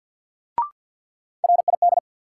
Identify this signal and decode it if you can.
Morse code, heard as a keyed tone, about 52 wpm, 700 Hz. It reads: RID
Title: Sets of 4 Words - New Word Drill - 50wpm